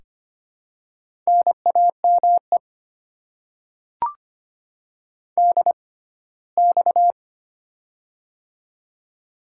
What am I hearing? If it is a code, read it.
Morse code, heard as a keyed tone, about 25 wpm, spaced as for 17 wpm, 700 Hz. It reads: NAME DX
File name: Top 90 QSO Elements - 6x Character Spacing - 25wpm